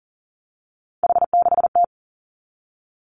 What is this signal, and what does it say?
Morse code, heard as a keyed tone, about 38 wpm, 700 Hz. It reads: H6T
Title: Three Challenging Letter-Number Combinations 40wpm